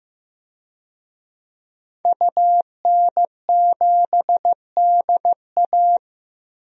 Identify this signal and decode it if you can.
Morse code, heard as a keyed tone, about 15 wpm, 700 Hz. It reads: UN7DA
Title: DXCC-100 Call Signs - 5 Characters - Repeated 2x - 15wpm